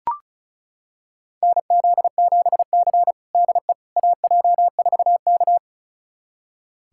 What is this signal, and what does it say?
Morse code, heard as a keyed tone, about 34 wpm, 700 Hz. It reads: NZ7C DE AJ4K
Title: USA Callsign DE Callsign - 4 Characters - 35wpm